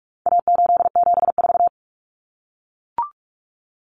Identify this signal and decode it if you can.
Morse code, heard as a keyed tone, about 44 wpm, 700 Hz. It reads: A874
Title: Common Cut Numbers - 4-Digit 45wpm